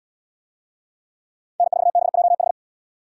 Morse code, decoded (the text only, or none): N4BJH